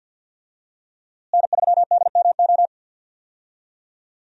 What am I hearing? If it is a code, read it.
Morse code, heard as a keyed tone, about 52 wpm, 700 Hz. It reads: N3DMO
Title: USA Call Signs - 5 Characters - Repeated 3x - 50wpm